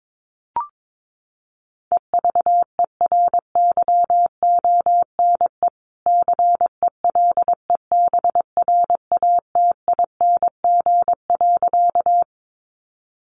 Morse code, heard as a keyed tone, about 22 wpm, 700 Hz. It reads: EVERYONE CELEBRATING.
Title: A Sound Of Thunder 22wpm